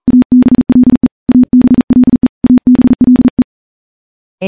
Morse code, heard as a keyed tone, about 40 wpm, 250 Hz. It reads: ABLE ABLE ABLE
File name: morse / morse-training-40WPM